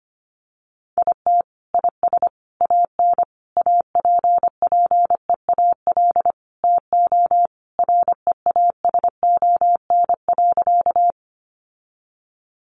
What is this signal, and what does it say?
Morse code, heard as a keyed tone, about 25 wpm, 700 Hz. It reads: IT IS AN APPEAL TO REASON.
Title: Sentences from Top 1300 Words 25wpm